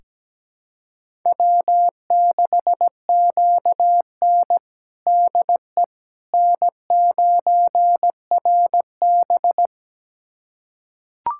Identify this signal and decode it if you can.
Morse code, heard as a keyed tone, about 17 wpm, 700 Hz. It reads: W6QN DE N9RB
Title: USA Callsign DE Callsign - 4 Characters - 17wpm